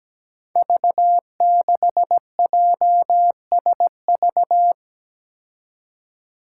Morse code, heard as a keyed tone, about 17 wpm, 700 Hz. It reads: V6JSV
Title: Five Challenging Letter-Number Combinations 17wpm